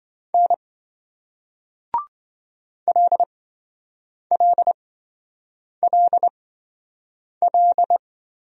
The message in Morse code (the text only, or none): N LLLL